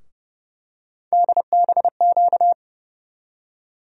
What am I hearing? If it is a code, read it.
Morse code, heard as a keyed tone, about 30 wpm, 700 Hz. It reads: DBQ
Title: USA Call Sign Postfixes - 3 Letters 30wpm